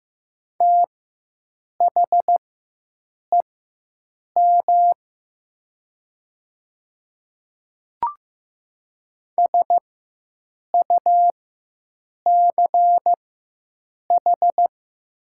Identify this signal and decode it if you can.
Morse code, heard as a keyed tone, about 15 wpm, spaced as for 7 wpm, 700 Hz. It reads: THEM SUCH
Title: 4x Character Spacing - Top 300 Words - 15wpm